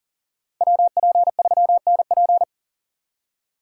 Morse code, heard as a keyed tone, about 49 wpm, 700 Hz. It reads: WP3NP
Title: Speed Racing - USA 2x2 Call Signs 40wpm